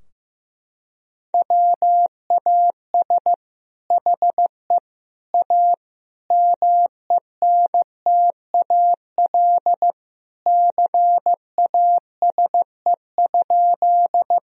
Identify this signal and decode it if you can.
Morse code, heard as a keyed tone, about 15 wpm, 700 Hz. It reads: WAS HE A MENTAL CASE?
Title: Sentences from Top 1000 Words - Review Part 4 - 15wpm